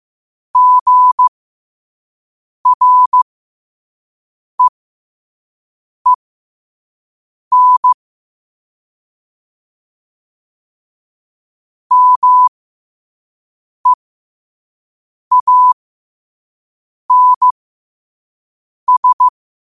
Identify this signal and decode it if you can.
Morse code, heard as a keyed tone, about 15 wpm, spaced as for 5 wpm, 1000 Hz. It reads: GREEN MEANS
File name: qotdfiles / 5wpm